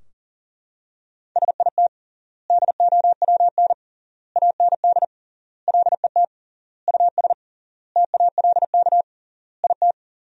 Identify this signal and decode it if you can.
Morse code, heard as a keyed tone, about 40 wpm, 700 Hz. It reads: SIT DOWN AND LET US TALK IT
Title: 3x Word-Spacing - Sentences from Top 500 Words - 40wpm